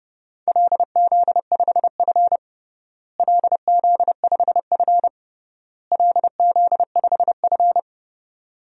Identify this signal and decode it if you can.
Morse code, heard as a keyed tone, about 30 wpm, 700 Hz. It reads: LZ5F LZ5F LZ5F
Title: DXCC-100 Call Signs - 4 Characters - Repeated 3x - 30wpm